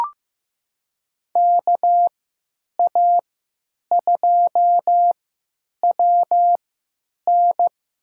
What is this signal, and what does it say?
Morse code, heard as a keyed tone, about 15 wpm, spaced as for 9 wpm, 700 Hz. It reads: KA2WN
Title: USA Call Signs - 3x Character Spacing - 15wpm